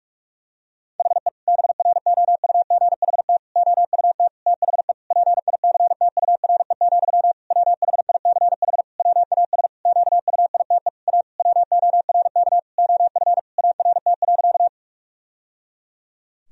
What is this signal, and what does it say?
Morse code, heard as a keyed tone, about 46 wpm, 700 Hz. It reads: HE BROUGHT OUT THE PICTURE, WHICH WAS QUITE A WORK OF ART.
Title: Sentences from Top 1000 Words - Review Part 3 - 45wpm